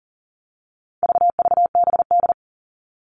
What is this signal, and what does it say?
Morse code, heard as a keyed tone, about 40 wpm, 700 Hz. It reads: VVBD
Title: All Cut Numbers - 4-Digit 40wpm